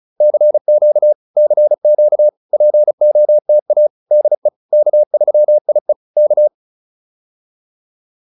Morse code, heard as a keyed tone, about 34 wpm, 600 Hz. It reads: CQ CQ POTA DE K3IE K
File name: POTA307 35